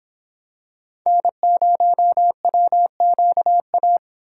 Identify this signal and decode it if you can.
Morse code, heard as a keyed tone, about 26 wpm, 700 Hz. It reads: N0WQA